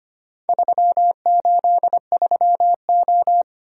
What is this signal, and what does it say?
Morse code, heard as a keyed tone, about 25 wpm, 700 Hz. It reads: 383O
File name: practice / 25wpm / Four Letter-Number Combinations - 25wpm